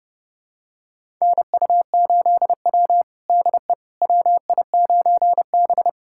Morse code, heard as a keyed tone, about 30 wpm, 700 Hz. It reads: NU8W DE WI9B
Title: USA Callsign DE Callsign - 4 Characters - 30wpm